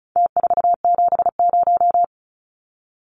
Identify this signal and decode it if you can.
Morse code, heard as a keyed tone, about 34 wpm, 700 Hz. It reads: T470
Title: Numbers - Contest Style - 4-Digit 35wpm